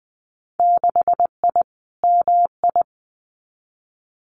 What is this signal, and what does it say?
Morse code, heard as a keyed tone, about 20 wpm, 700 Hz. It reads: 6I MI
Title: Winter Field Day Exchanges Encore 20wpm